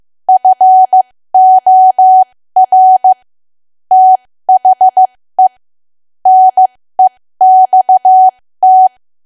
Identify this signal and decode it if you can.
Morse code, heard as a keyed tone, about 15 wpm, spaced as for 13 wpm, 750 Hz. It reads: FOR THE NEXT
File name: cwa / W1AW-Level2 / 13WPM